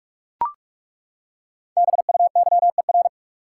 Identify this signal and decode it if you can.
Morse code, heard as a keyed tone, about 41 wpm, 700 Hz. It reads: BUYER
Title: Five Letter Words - 45wpm